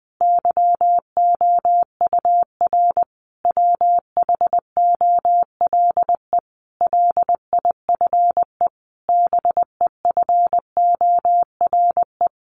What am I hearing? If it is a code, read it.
Morse code, heard as a keyed tone, about 20 wpm, 700 Hz. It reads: YOUR WHOLE LIFE BEFORE
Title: Sentences from Top 400 Words 20wpm